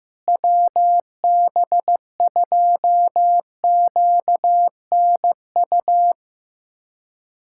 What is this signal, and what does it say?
Morse code, heard as a keyed tone, about 15 wpm, 700 Hz. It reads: WB2QNU
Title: K1USN SST QSO Practice - 15wpm